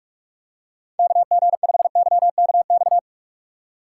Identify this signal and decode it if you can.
Morse code, heard as a keyed tone, about 43 wpm, 700 Hz. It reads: KG5YKX